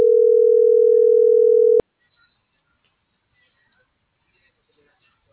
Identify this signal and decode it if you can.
Ringback tone (North America), 440 + 480 Hz.